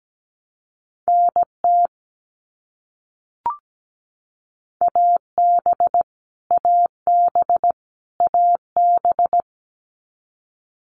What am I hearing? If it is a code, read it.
Morse code, heard as a keyed tone, about 17 wpm, 700 Hz. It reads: NT AB AB AB
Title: Canadian Provinces and Territories - Repeated 3x - 17wpm